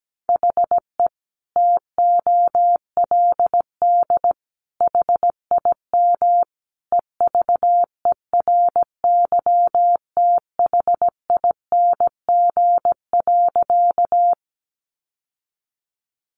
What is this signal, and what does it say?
Morse code, heard as a keyed tone, about 17 wpm, 700 Hz. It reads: HE TOLD HIM EVERYTHING.
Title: Sentences from Top 700 Words 17wpm